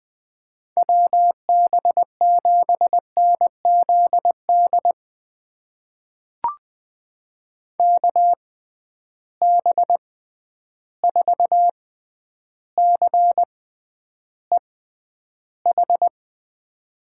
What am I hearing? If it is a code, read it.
Morse code, heard as a keyed tone, about 20 wpm, spaced as for 10 wpm, 700 Hz. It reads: WB7NZD KB4CEH
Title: USA Call Signs - 6 Characters - 6x Character Spacing - 20wpm